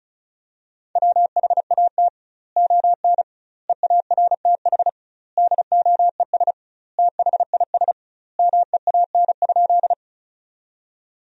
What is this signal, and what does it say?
Morse code, heard as a keyed tone, about 36 wpm, 700 Hz. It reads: WHAT ON EARTH DOES THIS MEAN?
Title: 2x Word-Spacing - Sentences from Top 200 Words - 35wpm